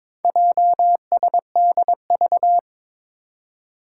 Morse code, heard as a keyed tone, about 22 wpm, 700 Hz. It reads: JSDV
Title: Four Challenging Letter-Number Combinations 22wpm